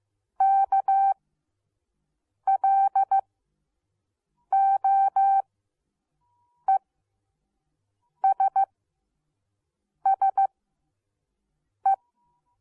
Morse code, heard as a keyed tone, about 15 wpm, spaced as for 5 wpm, 780 Hz. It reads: KLOESSE